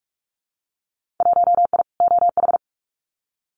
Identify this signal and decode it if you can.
Morse code, heard as a keyed tone, about 44 wpm, 700 Hz. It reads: 1I OH
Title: Winter Field Day Exchanges 45wpm